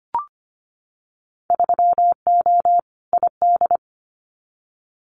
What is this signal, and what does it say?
Morse code, heard as a keyed tone, about 25 wpm, 690 Hz. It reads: 3O ID
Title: Winter Field Day Exchanges Encore 25wpm